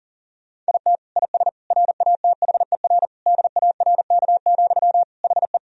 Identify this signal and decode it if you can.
Morse code, heard as a keyed tone, about 40 wpm, 700 Hz. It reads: IT IS RATHER DARK, HE
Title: Sentences from Top 1000 Words - Review Part 4 - 40wpm